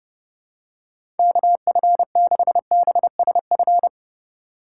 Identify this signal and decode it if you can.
Morse code, heard as a keyed tone, about 30 wpm, 700 Hz. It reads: KF6BSF
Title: USA Call Signs - 6 Characters - 2x Character Spacing - 30wpm